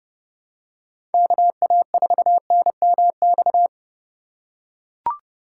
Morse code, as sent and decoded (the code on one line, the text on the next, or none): -.- .- ....- -. -- -..-
KA4NMX